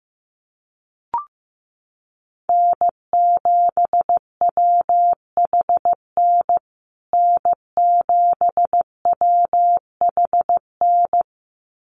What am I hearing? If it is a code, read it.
Morse code, heard as a keyed tone, about 15 wpm, 700 Hz. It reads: N7WHN N7WHN
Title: USA Call Signs Repeated 2x 15wpm